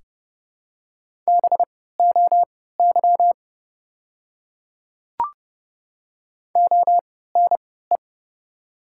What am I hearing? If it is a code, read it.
Morse code, heard as a keyed tone, about 31 wpm, spaced as for 17 wpm, 700 Hz. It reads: BOY ONE
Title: 3x Character Spacing - Top 200 Words - 30wpm